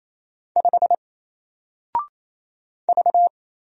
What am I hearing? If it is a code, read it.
Morse code, heard as a keyed tone, about 27 wpm, 700 Hz. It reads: 5 V
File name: Single-Letter-Number-28wpm